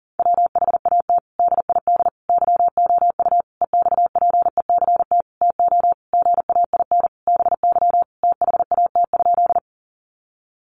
Morse code, heard as a keyed tone, about 40 wpm, 700 Hz. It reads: WHAT DID YOU EXPECT TO GAIN BY THAT?
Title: Sentences from Top 1100 Words 40wpm